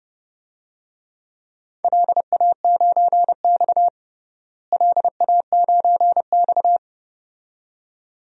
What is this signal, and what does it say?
Morse code, heard as a keyed tone, about 30 wpm, 700 Hz. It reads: LA9X LA9X
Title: DXCC-100 Call Signs - 4 Characters - Repeated 2x - 30wpm